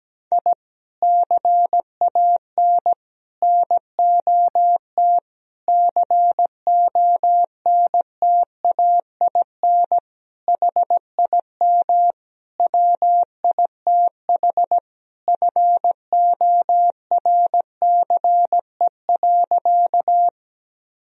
Morse code, heard as a keyed tone, about 17 wpm, 700 Hz. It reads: I CAN NOT CONTAIN HIM WITH FORCE.
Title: Sentences from Top 1000 Words - Review Part 3 - 17wpm